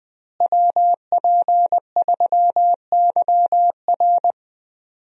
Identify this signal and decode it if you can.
Morse code, heard as a keyed tone, about 20 wpm, 700 Hz. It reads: WP3YR
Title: USA Call Signs - 5 Characters - Repeated 2x - 20wpm